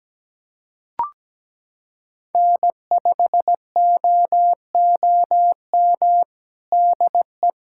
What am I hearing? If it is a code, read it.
Morse code, heard as a keyed tone, about 17 wpm, 700 Hz. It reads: N5OOM DE
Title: USA Callsign DE Callsign 17wpm